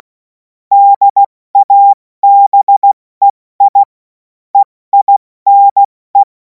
Morse code, heard as a keyed tone, about 16 wpm, spaced as for 14 wpm, 800 Hz. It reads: DABEI EINE
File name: Morsekurs / Witze / Witz 17. Februar 2026 70 F80